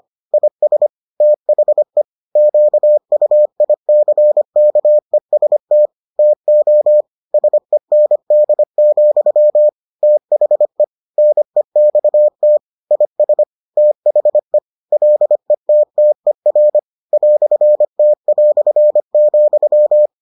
Morse code, heard as a keyed tone, about 25 wpm, 600 Hz. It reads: IS THE QUICKEST TO SEND, THE NEXT IS THE LETTER "T",